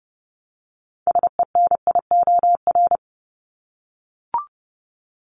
Morse code, heard as a keyed tone, about 30 wpm, 700 Hz. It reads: SENIOR
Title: Top 1751-2000 Words 30wpm